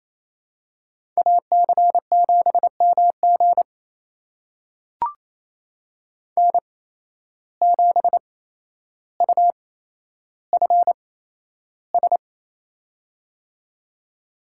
Morse code, heard as a keyed tone, about 27 wpm, spaced as for 12 wpm, 700 Hz. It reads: AC7MG N7UFS
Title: USA Call Signs - 5 Characters - 8x Character Spacing - 28wpm